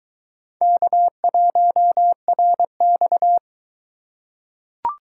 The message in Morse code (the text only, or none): K1RX